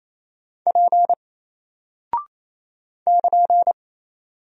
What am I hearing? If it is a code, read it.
Morse code, heard as a keyed tone, about 28 wpm, 700 Hz. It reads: P <KN>
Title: Single Character (Letter:Number:Punctuation:Prosign) Alt 28wpm